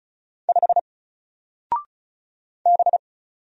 Morse code, heard as a keyed tone, about 35 wpm, 700 Hz. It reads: 5 B
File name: Single-Letter-Number-35wpm